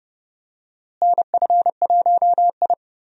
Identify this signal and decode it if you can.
Morse code, heard as a keyed tone, about 30 wpm, 700 Hz. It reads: NF1I
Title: USA Call Signs - 4 Characters - Repeated 2x - 30wpm